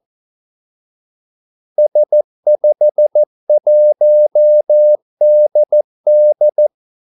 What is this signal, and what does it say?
Morse code, heard as a keyed tone, about 14 wpm, 600 Hz. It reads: S51DD
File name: Callsigns-14wpm